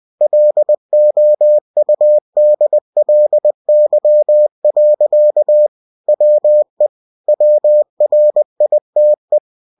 Morse code, heard as a keyed tone, about 20 wpm, 600 Hz. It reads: LOUDLY. WE WRITE
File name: cw-books / winnie-the-pooh / winnie-20wpm-page-0030